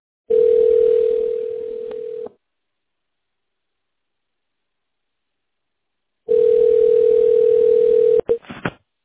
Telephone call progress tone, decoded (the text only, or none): ringback tone